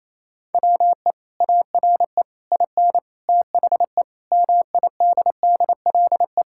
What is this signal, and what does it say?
Morse code, heard as a keyed tone, about 28 wpm, 700 Hz. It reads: WE ARE IN THE MIDDLE
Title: Sentences from Top 1000 Words - Review Part 6 - 28wpm